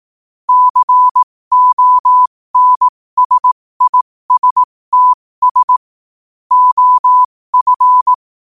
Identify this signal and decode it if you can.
Morse code, heard as a keyed tone, about 18 wpm, spaced as for 15 wpm, 1000 Hz. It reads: CONSISTS OF
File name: qotdfiles / 15wpm